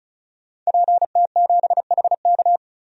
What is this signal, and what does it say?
Morse code, heard as a keyed tone, about 34 wpm, 700 Hz. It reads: PT7HK
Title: DXCC-100 Call Signs - 5 Characters - 35wpm